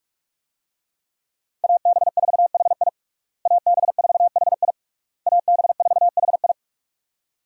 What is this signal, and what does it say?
Morse code, heard as a keyed tone, about 46 wpm, 700 Hz. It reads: AB4HI AB4HI AB4HI
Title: USA Call Signs - 5 Characters - Repeated 3x - 45wpm